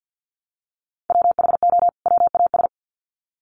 Morse code, heard as a keyed tone, about 48 wpm, 700 Hz. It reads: WHO WAS
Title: Sets of 2 Words 50wpm